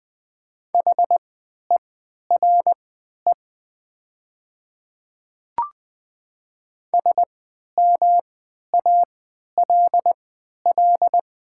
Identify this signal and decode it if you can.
Morse code, heard as a keyed tone, about 20 wpm, spaced as for 11 wpm, 700 Hz. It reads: HERE SMALL
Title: 3x Character Spacing - Top 200 Words - 20wpm